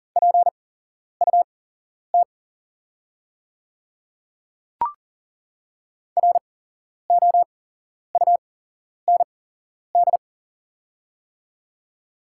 Morse code, heard as a keyed tone, about 40 wpm, spaced as for 11 wpm, 700 Hz. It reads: PUT ROUND
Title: 8x Character Spacing - Top 200 Words - 40wpm